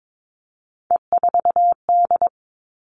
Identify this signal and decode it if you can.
Morse code, heard as a keyed tone, about 22 wpm, 700 Hz. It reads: E4D
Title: All Cut Numbers - 3-Digit 22wpm